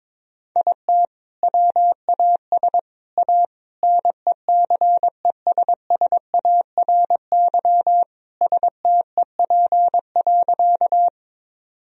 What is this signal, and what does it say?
Morse code, heard as a keyed tone, about 22 wpm, 700 Hz. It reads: IT WAS A NECESSARY STEP.
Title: Sentences from Top 1000 Words - Review Part 2 - 22wpm